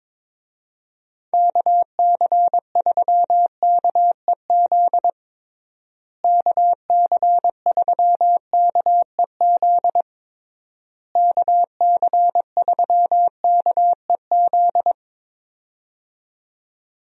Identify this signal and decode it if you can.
Morse code, heard as a keyed tone, about 22 wpm, 700 Hz. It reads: KC3KEZ KC3KEZ KC3KEZ
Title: USA Call Signs - 6 Characters - Repeated 3x - 22wpm